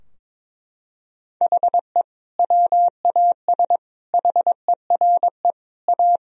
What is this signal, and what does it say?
Morse code, heard as a keyed tone, about 21 wpm, 700 Hz. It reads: HE WAS HERE A
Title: Sentences from Top 1000 Words - Review Part 3 - 22wpm